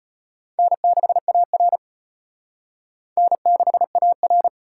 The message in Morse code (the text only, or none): N6AR N6AR